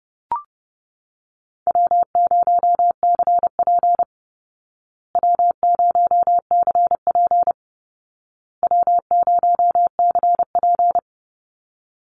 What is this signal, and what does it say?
Morse code, heard as a keyed tone, about 30 wpm, 700 Hz. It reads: W0CP W0CP W0CP
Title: Easy Call Signs Repeated 3x - 30wpm